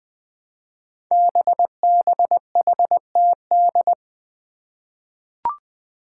Morse code, heard as a keyed tone, about 20 wpm, 700 Hz. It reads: BBHTD
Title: Five Challenging Letter-Number Combinations 20wpm